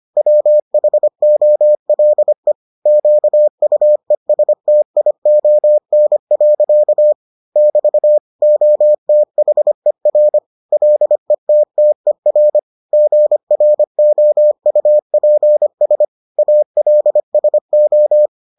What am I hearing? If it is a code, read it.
Morse code, heard as a keyed tone, about 25 wpm, 600 Hz. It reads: WHOLE QUESTION. <BT> OTHER LETTER GROUPS ALSO